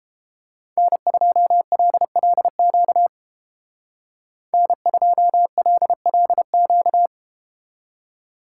Morse code, heard as a keyed tone, about 32 wpm, 700 Hz. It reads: N2LLQ N2LLQ